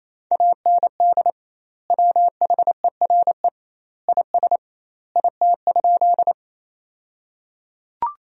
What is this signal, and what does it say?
Morse code, heard as a keyed tone, about 27 wpm, 700 Hz. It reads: AND WHERE IS IT?
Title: 2x Word-Spacing - Sentences from Top 200 Words - 28wpm